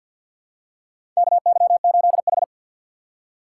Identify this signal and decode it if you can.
Morse code, heard as a keyed tone, about 53 wpm, 700 Hz. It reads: KY8H
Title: USA Call Signs - 4 Characters - 6x Character Spacing - 50wpm